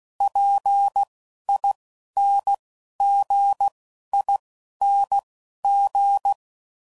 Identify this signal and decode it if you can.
Morse code, heard as a keyed tone, about 16 wpm, spaced as for 11 wpm, 800 Hz. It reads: PINGING
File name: Suffixes / Special INGs / Dble INGs